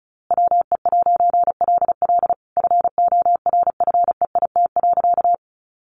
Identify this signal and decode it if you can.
Morse code, heard as a keyed tone, about 34 wpm, 700 Hz. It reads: WE'LL FORFEIT.
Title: A Sound Of Thunder 35wpm